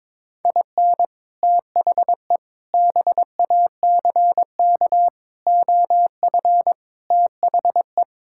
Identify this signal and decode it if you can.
Morse code, heard as a keyed tone, about 22 wpm, 700 Hz. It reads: IN THE BACK OF THE
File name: practice / 22wpm / Build A Phrase - 5-Word Phrases - 22wpm